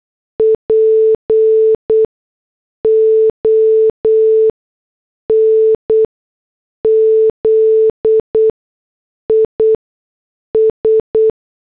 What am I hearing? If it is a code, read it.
Morse code, heard as a keyed tone, about 8 wpm, spaced as for 6 wpm, 440 Hz. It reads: PONZIS